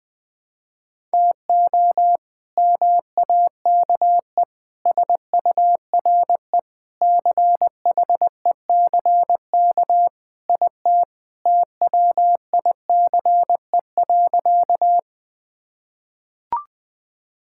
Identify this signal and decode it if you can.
Morse code, heard as a keyed tone, about 21 wpm, 700 Hz. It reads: TO MAKE SURE CHECK IT TWICE.
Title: Sentences with 3-Word Phrases - 20wpm